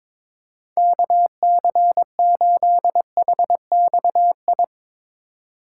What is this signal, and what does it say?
Morse code, heard as a keyed tone, about 22 wpm, 700 Hz. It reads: KC8HXI
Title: USA Call Signs - 6 Characters - 3x Character Spacing - 22wpm